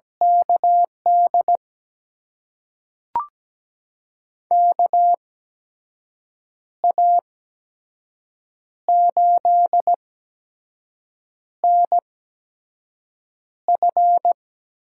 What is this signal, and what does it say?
Morse code, heard as a keyed tone, about 17 wpm, spaced as for 5 wpm, 700 Hz. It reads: KD KA8NF
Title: USA Call Signs - 6 Characters - 8x Character Spacing - 17wpm